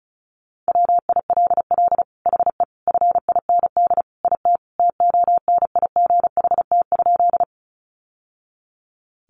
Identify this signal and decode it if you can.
Morse code, heard as a keyed tone, about 34 wpm, 700 Hz. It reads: WILL HE FIND IT TONIGHT?
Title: Sentences from Top 600 Words 35wpm